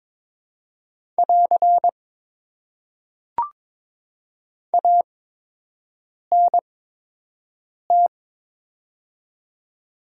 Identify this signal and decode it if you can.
Morse code, heard as a keyed tone, about 21 wpm, spaced as for 6 wpm, 700 Hz. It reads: <AR> ANT
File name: Top 90 QSO Elements - 8x Character Spacing - 22wpm